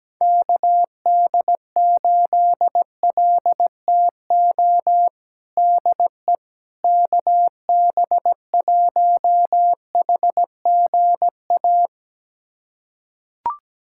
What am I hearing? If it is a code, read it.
Morse code, heard as a keyed tone, about 17 wpm, 700 Hz. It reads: KD8LTO DE KB1HGA
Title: USA Callsign DE Callsign - 6 Characters - 17wpm